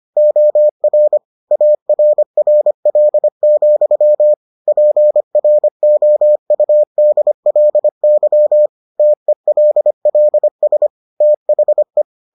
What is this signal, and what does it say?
Morse code, heard as a keyed tone, about 25 wpm, 600 Hz. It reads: OR ARRL, PROUDLY TELLS THE